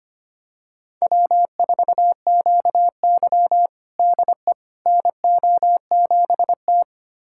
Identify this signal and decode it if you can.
Morse code, heard as a keyed tone, about 25 wpm, 700 Hz. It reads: W4QY DE NO7T
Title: USA Callsign DE Callsign - 4 Characters - 25wpm